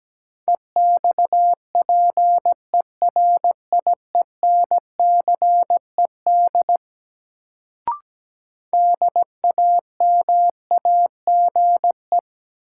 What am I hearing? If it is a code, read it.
Morse code, heard as a keyed tone, about 17 wpm, 700 Hz. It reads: EXPERIENCED DAMAGE
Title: Sentences from Top 1100 Words - New Word Drill - 17wpm